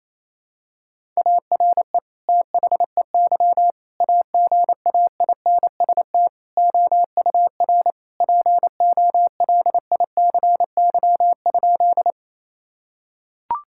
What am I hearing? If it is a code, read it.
Morse code, heard as a keyed tone, about 28 wpm, 700 Hz. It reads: ARE THEY AGAINST OUR POLICY?
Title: Sentences from Top 1000 Words - Review Part 5 - 28wpm